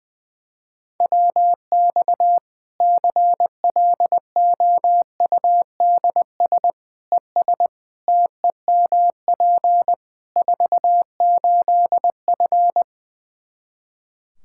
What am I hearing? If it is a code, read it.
Morse code, heard as a keyed tone, about 20 wpm, 700 Hz. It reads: WX CLOUDS ES TEMP 48F
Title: Practice QSO - Exchange 2 - Info - 20wpm